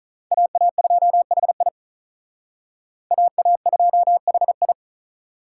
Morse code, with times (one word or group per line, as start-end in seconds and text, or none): 0.3-1.7 AA2HI
3.1-4.7 AA2HI